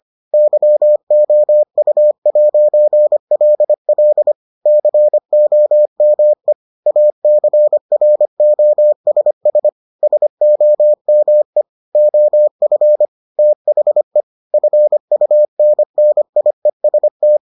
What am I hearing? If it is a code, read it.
Morse code, heard as a keyed tone, about 25 wpm, 600 Hz. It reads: YOU'LL COME ACROSS SOME OF THE FUNNIEST